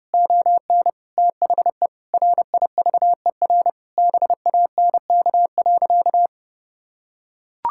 Morse code, heard as a keyed tone, about 30 wpm, 700 Hz. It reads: ON THE RIVER BANK.
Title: Sentences from Top 1000 Words - Review Part 6 - 30wpm